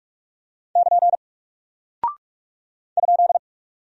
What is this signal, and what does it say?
Morse code, heard as a keyed tone, about 44 wpm, 700 Hz. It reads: <KN> ?
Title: Single Character (Letter:Number:Punctuation:Prosign) Alt 45wpm